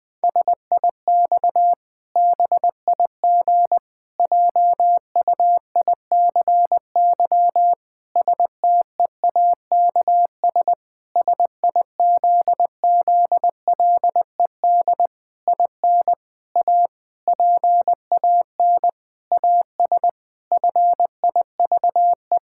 Morse code, heard as a keyed tone, about 20 wpm, 700 Hz. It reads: SIX BIG JUICY STEAKS SIZZLED IN A PAN AS FIVE
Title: Pangram Challenge v2 - 20wpm